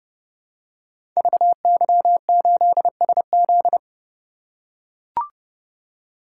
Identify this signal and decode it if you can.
Morse code, heard as a keyed tone, about 30 wpm, 700 Hz. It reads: VY8SZ
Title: Code Groups - 5 Letter:Numbers - 30wpm